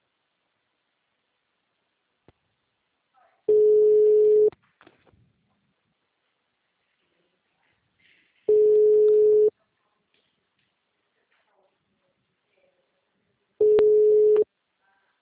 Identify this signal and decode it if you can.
Ringback tone (Europe, 425 Hz), with speech.
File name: out- -2024- - - .141